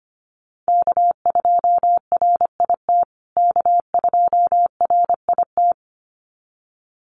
Morse code, heard as a keyed tone, about 25 wpm, 700 Hz. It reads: K2RIT K2RIT
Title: USA Call Signs Repeated 2x 25wpm